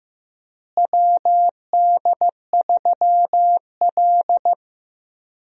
Morse code, heard as a keyed tone, about 15 wpm, 700 Hz. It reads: WD3L
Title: USA Call Signs - 4 Characters - Repeated 2x - 15wpm